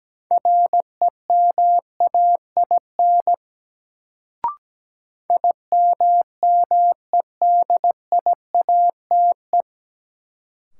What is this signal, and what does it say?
Morse code, heard as a keyed tone, about 17 wpm, 700 Hz. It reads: REMAIN IMMEDIATE
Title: Sentences from Top 1200 Words - New Word Drill - 17wpm